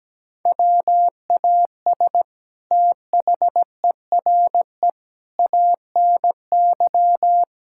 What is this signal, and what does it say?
Morse code, heard as a keyed tone, about 17 wpm, 700 Hz. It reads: WAS THERE ANY
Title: Sentences from Top 1600 Words 17wpm